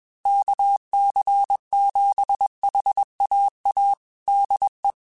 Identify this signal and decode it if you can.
Morse code, heard as a keyed tone, about 21 wpm, 800 Hz. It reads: KC7HAA DE
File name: QSOs / QSOs-mp3s / kc7haa@20